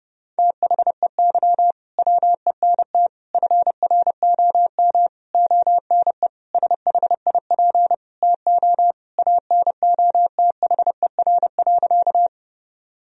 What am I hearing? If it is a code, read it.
Morse code, heard as a keyed tone, about 32 wpm, 700 Hz. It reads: THEY WENT FROM ONE SHIP TO ANOTHER.
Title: Sentences from Top 1000 Words - Review Part 2 - 30wpm